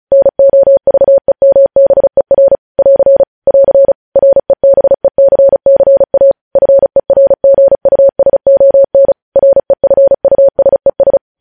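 Morse code, heard as a keyed tone, about 36 wpm, 550 Hz. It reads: NOVEMBER <AR> <AR> REBECCA FERGUSON REFUSES
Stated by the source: News headlines sent in Morse code at 35 WPM.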